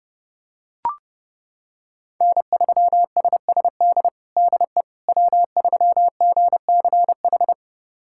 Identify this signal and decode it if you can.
Morse code, heard as a keyed tone, about 30 wpm, 700 Hz. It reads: N3SSD DE W3GCH
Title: USA Callsign DE Callsign - 5 Characters - 30wpm